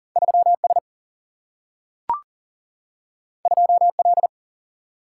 Morse code, heard as a keyed tone, about 40 wpm, 700 Hz. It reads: 3 S 2 L
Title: Single-Letter Single-Number 40wpm